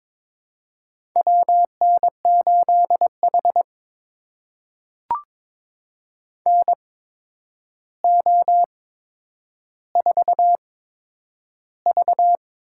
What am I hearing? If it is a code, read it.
Morse code, heard as a keyed tone, about 22 wpm, spaced as for 9 wpm, 700 Hz. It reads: WN8H NO4V
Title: USA Call Signs - 4 Characters - 8x Character Spacing - 22wpm